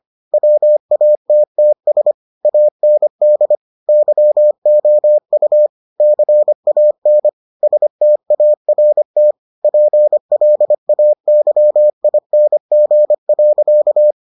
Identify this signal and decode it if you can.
Morse code, heard as a keyed tone, about 25 wpm, 600 Hz. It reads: WATTS AND YOU CAN START PLAYING.